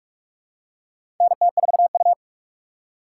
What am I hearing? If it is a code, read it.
Morse code, heard as a keyed tone, about 46 wpm, 700 Hz. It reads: NT4U
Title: USA Call Signs - 4 Characters - 2x Character Spacing - 45wpm